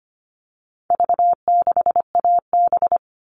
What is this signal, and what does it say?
Morse code, heard as a keyed tone, about 25 wpm, 700 Hz. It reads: V6AB
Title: All Cut Numbers - 4-Digit 25wpm